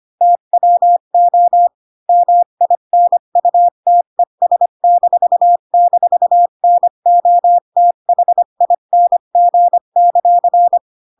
Morse code, heard as a keyed tone, about 25 wpm, 700 Hz. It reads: TWO MINUTES--NOTHING;
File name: Pinocchio / Pinocchio ch1 20 25 ik1hge